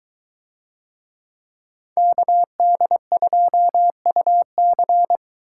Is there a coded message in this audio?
Yes: Morse code from 2.0 to 5.2 s, heard as a keyed tone, about 22 wpm, 700 Hz. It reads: KD2UC